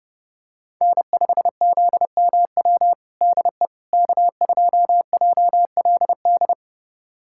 Morse code, heard as a keyed tone, about 30 wpm, 700 Hz. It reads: N5ZMW DE K2JLD
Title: USA Callsign DE Callsign - 5 Characters - 30wpm